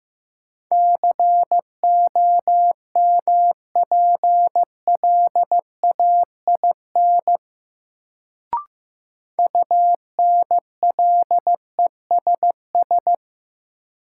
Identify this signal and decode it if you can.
Morse code, heard as a keyed tone, about 16 wpm, 700 Hz. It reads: COMPLAIN UNLESS
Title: Sentences from Top 1300 Words - New Word Drill Alt - 15wpm